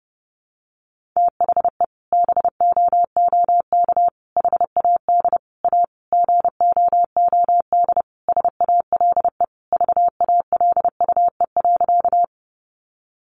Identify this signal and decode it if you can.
Morse code, heard as a keyed tone, about 30 wpm, 700 Hz. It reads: THE BOOK HAD A GOOD SALE VALUE.
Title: Sentences from Top 1400 Words 30wpm